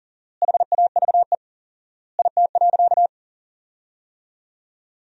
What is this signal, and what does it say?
Morse code, heard as a keyed tone, about 40 wpm, 700 Hz. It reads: HAVE IT.
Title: 4x Word-Spacing - Sentences from Top 100 Words - 40wpm